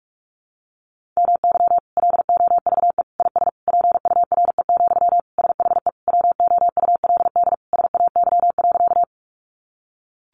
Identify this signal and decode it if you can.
Morse code, heard as a keyed tone, about 44 wpm, 700 Hz. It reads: MY LOVE IS PURE, SHE WOULD SAY.
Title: Sentences from Top 1500 Words 45wpm